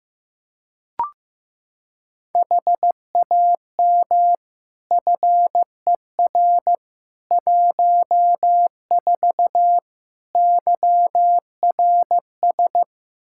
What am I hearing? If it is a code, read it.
Morse code, heard as a keyed tone, about 15 wpm, 700 Hz. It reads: HAM FER 14 YRS
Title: K1USN SST QSO Practice 15wpm